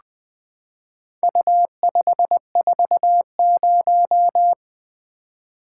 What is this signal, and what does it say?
Morse code, heard as a keyed tone, about 21 wpm, 700 Hz. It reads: U540
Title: Four Letter-Number Combinations - 20wpm